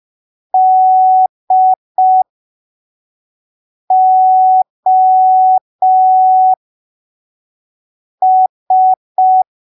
Morse code, heard as a keyed tone, about 5 wpm, spaced as for 3 wpm, 750 Hz. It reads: DOS
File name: arquivos / texto / texto 13 05ppm